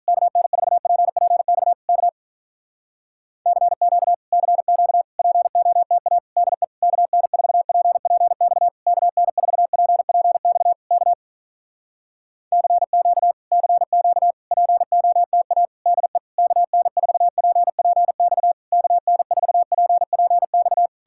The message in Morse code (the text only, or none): KN4PPX K CQ CQ POTA DE KN4PPX KN4PPX K CQ CQ POTA DE KN4PPX KN4PPX